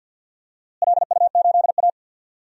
Morse code, heard as a keyed tone, about 49 wpm, 700 Hz. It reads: LU8A
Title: DXCC-100 Call Signs - 4 Characters - Repeated 3x - 50wpm